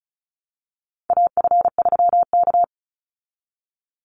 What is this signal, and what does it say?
Morse code, heard as a keyed tone, about 34 wpm, 700 Hz. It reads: AF3K
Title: Call Signs Repeated 2x - 35wpm